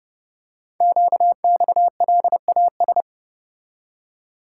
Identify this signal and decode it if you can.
Morse code, heard as a keyed tone, about 28 wpm, 700 Hz. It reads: QXLAS
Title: Code Groups - 5 Letters - 30wpm